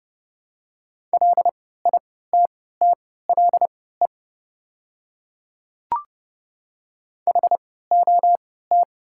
Morse code, heard as a keyed tone, about 30 wpm, spaced as for 17 wpm, 690 Hz. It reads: LITTLE HOT
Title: 3x Character Spacing - Top 500 Words - 30wpm